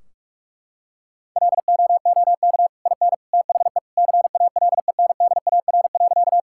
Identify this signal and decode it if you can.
Morse code, heard as a keyed tone, about 46 wpm, 700 Hz. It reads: LOOK IN THE CALENDAR.
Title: Sentences from Top 1600 Words 45wpm